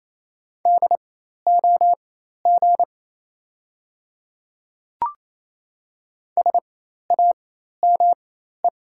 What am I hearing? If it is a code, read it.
Morse code, heard as a keyed tone, about 28 wpm, spaced as for 13 wpm, 700 Hz. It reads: DOG SAME
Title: 4x Character Spacing - Top 500 Words - 28wpm